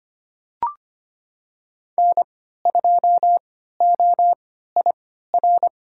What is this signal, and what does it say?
Morse code, heard as a keyed tone, about 25 wpm, spaced as for 14 wpm, 700 Hz. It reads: N2OIR
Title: USA Call Signs - 3x Character Spacing - 25wpm